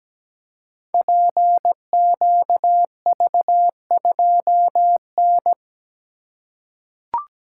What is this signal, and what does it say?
Morse code, heard as a keyed tone, about 17 wpm, 700 Hz. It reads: PQV2N
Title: Code Groups - 5 Letter:Numbers - 17wpm